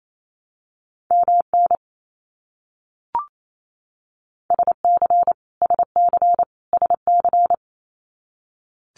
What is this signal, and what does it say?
Morse code, heard as a keyed tone, about 28 wpm, 700 Hz. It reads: MN SC SC SC
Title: US States Repeated 3x 28wpm